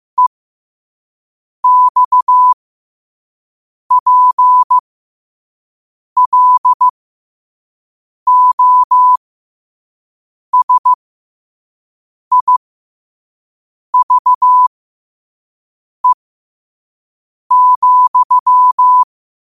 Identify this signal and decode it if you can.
Morse code, heard as a keyed tone, about 15 wpm, spaced as for 5 wpm, 1000 Hz. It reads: EXPLOSIVE,
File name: qotdfiles / 5wpm